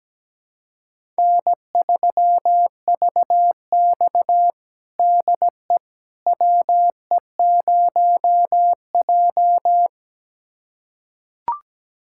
Morse code, heard as a keyed tone, about 17 wpm, 700 Hz. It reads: N3VX DE WE0J
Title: USA Callsign DE Callsign - 4 Characters - 17wpm